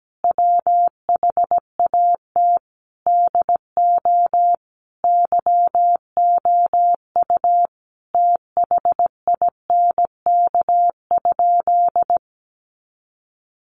Morse code, heard as a keyed tone, about 17 wpm, 700 Hz. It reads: WHAT DO YOU THINK?
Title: Sentences from Top 200 Words 17wpm